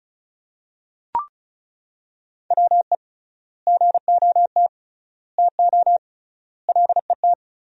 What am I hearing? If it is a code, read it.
Morse code, heard as a keyed tone, about 36 wpm, 700 Hz. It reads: WE GOT TO LET
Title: 3x Word-Spacing - Sentences from Top 300 Words - 35wpm